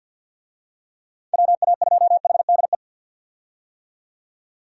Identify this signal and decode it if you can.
Morse code, heard as a keyed tone, about 53 wpm, 700 Hz. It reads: WA2HDE
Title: USA Call Signs - 6 Characters - 50wpm